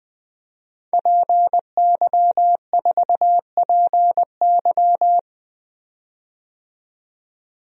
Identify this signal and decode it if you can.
Morse code, heard as a keyed tone, about 20 wpm, 700 Hz. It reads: PY4PY
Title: DXCC-100 Call Signs - 5 Characters - 20wpm